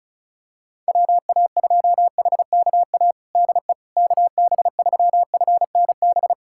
Morse code, heard as a keyed tone, about 35 wpm, 700 Hz. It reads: WA2HKA DE KB3FNB
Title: USA Callsign DE Callsign - 6 Characters - 35wpm